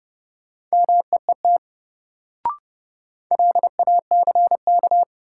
Morse code, heard as a keyed tone, about 30 wpm, 700 Hz. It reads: MEET LACK
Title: Sentences from Top 1100 Words - New Word Drill Alt - 30wpm